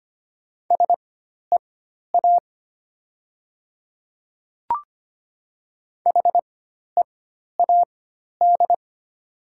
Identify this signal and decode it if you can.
Morse code, heard as a keyed tone, about 25 wpm, spaced as for 12 wpm, 700 Hz. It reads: SEA HEAD
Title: 4x Character Spacing - Top 500 Words - 25wpm